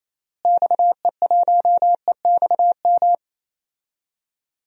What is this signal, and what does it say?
Morse code, heard as a keyed tone, about 28 wpm, 700 Hz. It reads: XE1EXM
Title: DXCC-100 Call Signs - 6 Characters - Repeated 3x - 28wpm